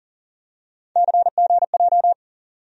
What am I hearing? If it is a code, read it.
Morse code, heard as a keyed tone, about 40 wpm, 700 Hz. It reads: CGJ
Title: USA Call Sign Postfixes - 3 Letters 40wpm